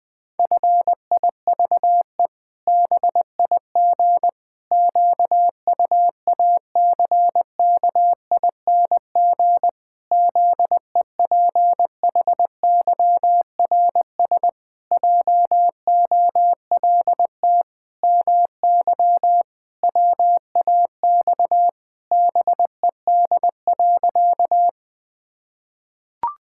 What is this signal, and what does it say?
Morse code, heard as a keyed tone, about 20 wpm, 700 Hz. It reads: FIVE BIG QUACKING ZEPHYRS JOLT MY WAX BED.
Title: Pangram Challenge v2 - 20wpm